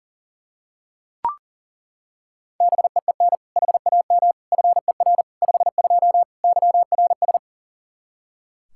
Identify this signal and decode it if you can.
Morse code, heard as a keyed tone, about 37 wpm, 700 Hz. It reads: BEEN HAM FER 52 YRS
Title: Practice QSO - Exchange 3 - More Info - 40wpm